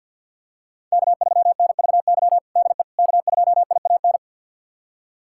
Morse code, heard as a keyed tone, about 52 wpm, 700 Hz. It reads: K3NVY DE K2IAN
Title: USA Callsign DE Callsign - 5 Characters - 50wpm